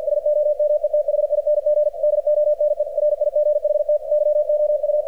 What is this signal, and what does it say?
Morse code, heard as a keyed tone, about 48 wpm, 600 Hz, with radio noise. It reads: SOMETHING DONE RIGHT YOU
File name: archived / CRSnet